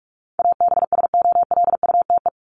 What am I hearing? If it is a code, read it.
Morse code, heard as a keyed tone, about 44 wpm, 700 Hz. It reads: ABSOLUTE
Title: LY Suffix - Top 100 Words with Base Words 45wpm